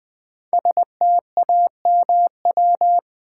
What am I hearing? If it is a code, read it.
Morse code, heard as a keyed tone, about 20 wpm, 700 Hz. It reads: STAMW
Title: Code Groups - 5 Letters - 20wpm